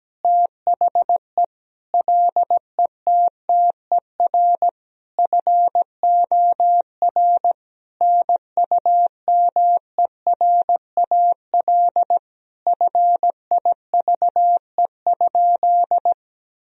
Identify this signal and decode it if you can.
Morse code, heard as a keyed tone, about 17 wpm, 700 Hz. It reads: THE LETTER FOR NUMERAL FIVE?
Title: Sentences from Top 1000 Words - Review Part 2 - 17wpm